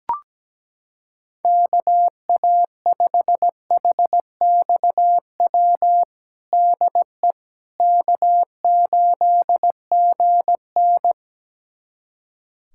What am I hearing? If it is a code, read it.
Morse code, heard as a keyed tone, about 17 wpm, 700 Hz. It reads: KA5HXW DE K8GN
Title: USA Callsign DE Callsign 17wpm